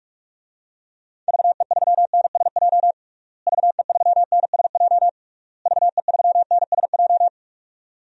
Morse code, heard as a keyed tone, about 41 wpm, 700 Hz. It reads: VE3NSJ VE3NSJ VE3NSJ
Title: DXCC-100 Call Signs - 6 Characters - Repeated 3x - 45wpm